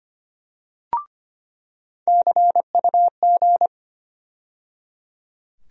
Morse code, heard as a keyed tone, about 25 wpm, 700 Hz. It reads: CUG
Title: USA Call Sign Postfixes - 3 Letters 25wpm